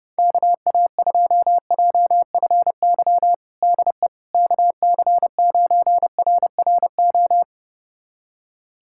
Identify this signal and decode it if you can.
Morse code, heard as a keyed tone, about 32 wpm, 690 Hz. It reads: KA2JFY DE KC9RRO
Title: USA Callsign DE Callsign - 6 Characters - 30wpm